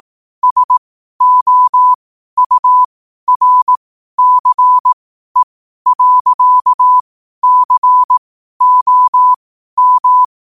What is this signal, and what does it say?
Morse code, heard as a keyed tone, about 18 wpm, spaced as for 12 wpm, 1000 Hz. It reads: SOURCE.COM
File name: qotdfiles / 12wpm